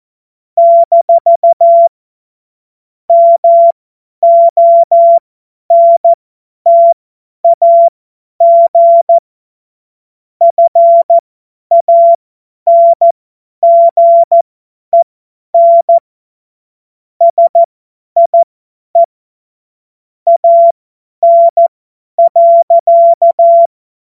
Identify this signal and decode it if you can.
Morse code, heard as a keyed tone, about 14 wpm, spaced as for 10 wpm, 680 Hz. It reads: - MONTAG FANGEN SIE AN.
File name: Morsekurs / Witze / Witz 17. Januar 2023 50 F70